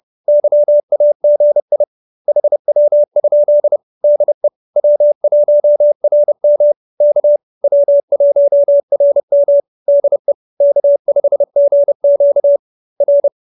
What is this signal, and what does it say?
Morse code, heard as a keyed tone, about 30 wpm, 570 Hz. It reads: YAGI HW? DE W1RM K W1RM DE K5GQ R